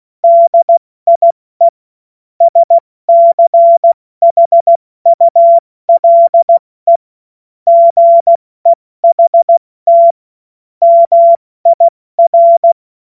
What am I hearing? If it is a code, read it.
Morse code, heard as a keyed tone, about 16 wpm, 680 Hz. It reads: DIE SCHULE GEHT MIR
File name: Morsekurs / Witze / Witz 14. September 2021, 70 F80